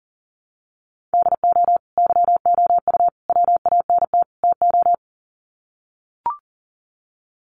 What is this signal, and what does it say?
Morse code, heard as a keyed tone, about 40 wpm, 700 Hz. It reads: DO YOU WANT TO
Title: Sets of 4 Words 40wpm